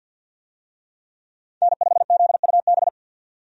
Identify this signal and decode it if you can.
Morse code, heard as a keyed tone, about 52 wpm, 700 Hz. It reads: N5ZUB
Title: USA Call Signs - 5 Characters - 50wpm